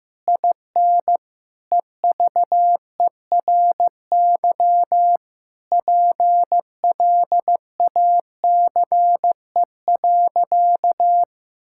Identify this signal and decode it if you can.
Morse code, heard as a keyed tone, about 15 wpm, 700 Hz. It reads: IN EVERY PLACE.
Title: Sentences from Top 1600 Words 15wpm